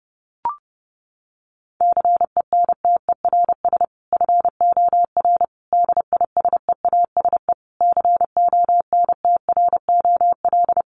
Morse code, heard as a keyed tone, about 30 wpm, 700 Hz. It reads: CENTERS FOR DISEASE CONTROL
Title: Sets of 5 Words Encore2 30wpm